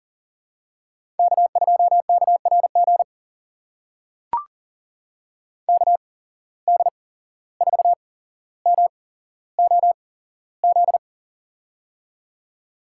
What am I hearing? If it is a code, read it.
Morse code, heard as a keyed tone, about 40 wpm, spaced as for 16 wpm, 690 Hz. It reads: K2KRG KD4MOZ
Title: USA Call Signs - 8x Character Spacing - 40wpm